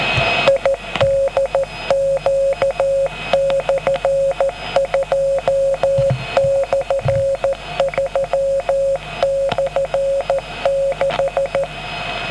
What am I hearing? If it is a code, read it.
Morse code, heard as a keyed tone, about 13 wpm, 550 Hz, with noise. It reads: IDQ/2/3/6